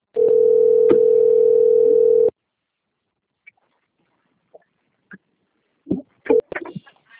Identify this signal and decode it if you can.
Ringback tone (North America), 440 + 480 Hz.
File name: rg-1600-unknown- - - .36467